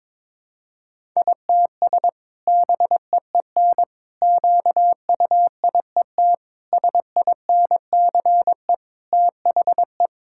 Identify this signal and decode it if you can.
Morse code, heard as a keyed tone, about 22 wpm, 700 Hz. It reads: ITS BEEN QUIET SINCE THE
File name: practice / 22wpm / Sentences with 2-Word Phrases - 22wpm